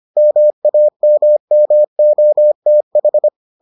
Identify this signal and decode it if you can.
Morse code, heard as a keyed tone, about 25 wpm, 600 Hz. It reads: MAMMOTH